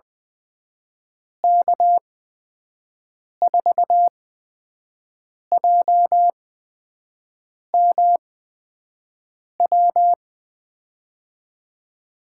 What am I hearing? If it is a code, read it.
Morse code, heard as a keyed tone, about 20 wpm, spaced as for 5 wpm, 700 Hz. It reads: K4JMW
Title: USA Call Signs - 6 Characters - 8x Character Spacing - 20wpm